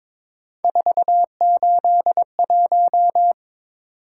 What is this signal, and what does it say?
Morse code, heard as a keyed tone, about 22 wpm, 700 Hz. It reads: 481
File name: practice / 22wpm / Three Letter-Number Combinations - 22wpm